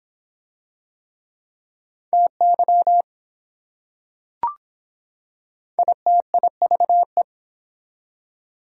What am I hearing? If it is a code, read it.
Morse code, heard as a keyed tone, about 24 wpm, 700 Hz. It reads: TY ITIVE